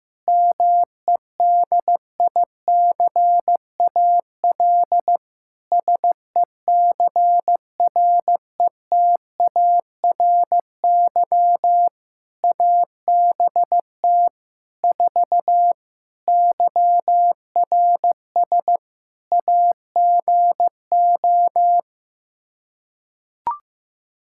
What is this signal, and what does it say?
Morse code, heard as a keyed tone, about 15 wpm, 700 Hz. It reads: MEDICAL SECRETARY ABT 4 YRS AGO
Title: Practice QSO - Exchange 3 - More Info - 15wpm